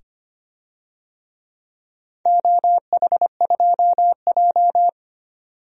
Morse code, heard as a keyed tone, about 25 wpm, 700 Hz. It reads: OH2J
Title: DXCC-100 Call Signs - 4 Characters - 25wpm